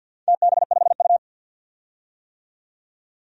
Morse code, heard as a keyed tone, about 53 wpm, 700 Hz. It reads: TB5U
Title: DXCC-100 Call Signs - 4 Characters - 50wpm